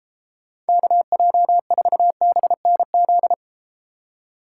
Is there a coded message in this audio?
Yes: Morse code from 0.7 to 3.3 s, heard as a keyed tone, about 30 wpm, 700 Hz. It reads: KJ4BNZ